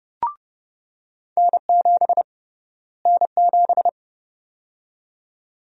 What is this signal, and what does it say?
Morse code, heard as a keyed tone, about 30 wpm, 700 Hz. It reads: N7 N7
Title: USA Call Sign Prefixes - Repeated 2x - 30wpm